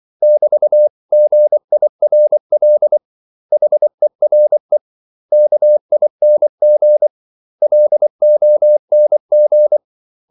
Morse code, heard as a keyed tone, about 24 wpm, 600 Hz. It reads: <BT>GIRL HERE KING LONG
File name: CW / Practice / 500MostCommonEnglishWords24